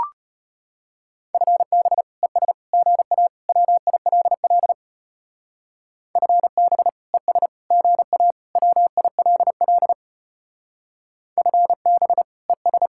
Morse code, heard as a keyed tone, about 32 wpm, 700 Hz. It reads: FB ES GA WILL FB ES GA WILL FB ES